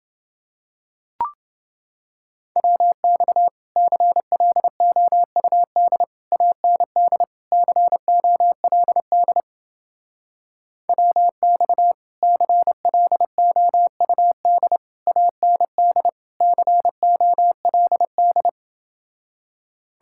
Morse code, heard as a keyed tone, about 28 wpm, 700 Hz. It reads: WX CLOUD AND COLD WX CLOUD AND COLD